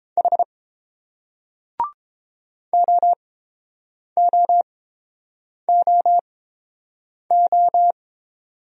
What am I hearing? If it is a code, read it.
Morse code, heard as a keyed tone, about 31 wpm, spaced as for 7 wpm, 700 Hz. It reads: H OOOO